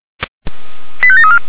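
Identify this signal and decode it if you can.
Five-tone selective call (ZVEI), heard as quick runs of five tones, tones of about 70 ms. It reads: 85734
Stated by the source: ZVEI five-tone selective call 89170 received with an SDR.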